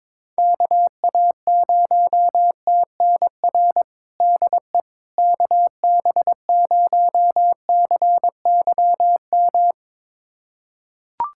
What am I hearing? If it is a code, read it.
Morse code, heard as a keyed tone, about 22 wpm, 700 Hz. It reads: KA0TNR DE KB0CYM
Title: USA Callsign DE Callsign 22wpm